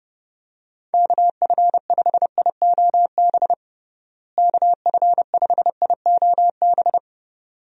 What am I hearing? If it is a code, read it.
Morse code, heard as a keyed tone, about 30 wpm, 700 Hz. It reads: KF5IOB KF5IOB
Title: USA Call Signs - 6 Characters - Repeated 2x - 30wpm